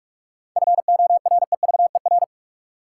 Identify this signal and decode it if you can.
Morse code, heard as a keyed tone, about 41 wpm, 700 Hz. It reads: FOREVER
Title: Sentences from Top 1600 Words - New Word Drill Alt - 45wpm